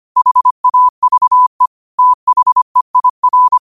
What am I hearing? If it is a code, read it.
Morse code, heard as a keyed tone, about 25 wpm, 1000 Hz. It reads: HAVE THEIR
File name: qotdfiles / 25wpm